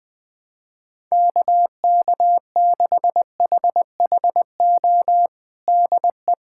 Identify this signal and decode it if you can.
Morse code, heard as a keyed tone, about 20 wpm, 700 Hz. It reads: KK6HHO DE
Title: USA Callsign DE Callsign 20wpm